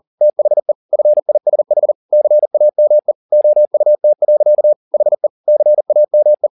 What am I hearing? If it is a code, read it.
Morse code, heard as a keyed tone, about 40 wpm, 600 Hz. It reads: THE FISH CAME OUT. HE CAME